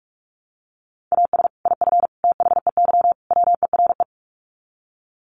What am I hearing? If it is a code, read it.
Morse code, heard as a keyed tone, about 44 wpm, 700 Hz. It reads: AS IF THEY WERE
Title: Sets of 4 Words 45wpm